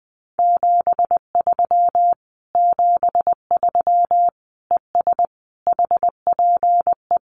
Morse code, heard as a keyed tone, about 20 wpm, 700 Hz. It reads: 73 73 ES HPE
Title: Common QSO Phrases Encore 20wpm